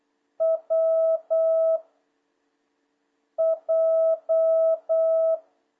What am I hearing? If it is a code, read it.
Morse code, heard as a keyed tone, about 8 wpm, 650 Hz. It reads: W J
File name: jh8cbh / 2026cw